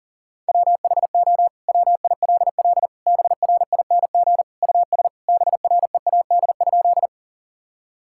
Morse code, heard as a keyed tone, about 40 wpm, 700 Hz. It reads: WHO WILL BRING US BREAD?
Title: Sentences from Top 1600 Words 40wpm